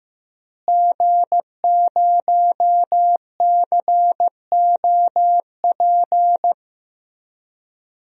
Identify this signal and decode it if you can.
Morse code, heard as a keyed tone, about 15 wpm, 700 Hz. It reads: G0COP
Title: DXCC-100 Call Signs - 5 Characters - Repeated 3x - 15wpm